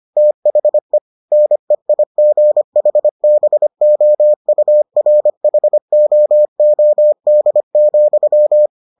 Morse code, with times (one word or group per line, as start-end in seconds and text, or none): 0.2-1.0 THE
1.3-8.7 NEIGHBOURHOOD,